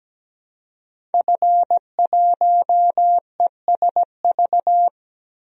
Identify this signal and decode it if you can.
Morse code, heard as a keyed tone, about 17 wpm, 700 Hz. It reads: F1ESV
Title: DXCC-100 Call Signs - 5 Characters - 17wpm